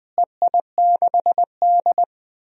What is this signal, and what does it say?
Morse code, heard as a keyed tone, about 20 wpm, 700 Hz. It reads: EI6D
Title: DXCC-100 Call Signs - 4 Characters - Repeated 2x - 20wpm